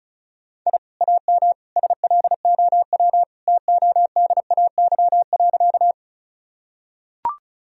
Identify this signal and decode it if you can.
Morse code, heard as a keyed tone, about 36 wpm, 700 Hz. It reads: I AM SLOW TODAY.
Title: Sentences from Top 1000 Words - Review Part 2 - 35wpm